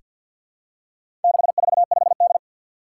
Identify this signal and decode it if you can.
Morse code, heard as a keyed tone, about 47 wpm, 700 Hz. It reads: 645D
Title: Four Challenging Letter-Number Combinations 50wpm